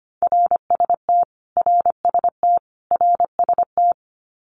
Morse code, heard as a keyed tone, about 25 wpm, 700 Hz. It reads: RST RST RST
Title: Abbreviations Repeated 3x - 25wpm